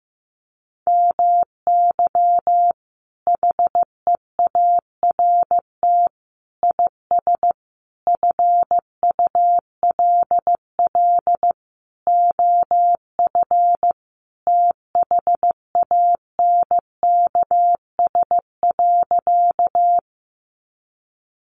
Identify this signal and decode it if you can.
Morse code, heard as a keyed tone, about 15 wpm, 700 Hz. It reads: MY HEART IS FULL OF THANKS.
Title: Sentences from Top 1200 Words 15wpm